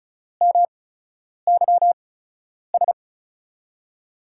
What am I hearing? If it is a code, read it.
Morse code, heard as a keyed tone, about 36 wpm, spaced as for 10 wpm, 700 Hz. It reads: MYS
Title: USA Call Signs - 5 Characters - 8x Character Spacing - 35wpm